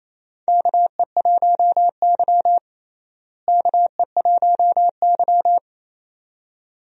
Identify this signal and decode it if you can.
Morse code, heard as a keyed tone, about 28 wpm, 700 Hz. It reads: KE1Y KE1Y
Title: USA Call Signs - 4 Characters - Repeated 2x - 28wpm